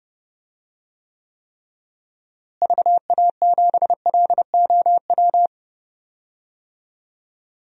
Morse code, heard as a keyed tone, about 32 wpm, 700 Hz. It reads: VA7LOW
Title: DXCC-100 Call Signs - 6 Characters - Repeated 3x - 30wpm